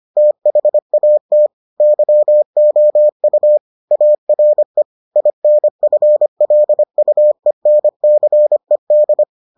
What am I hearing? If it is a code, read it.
Morse code, heard as a keyed tone, about 25 wpm, 600 Hz. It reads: THAT YOU ARE INFLUENCED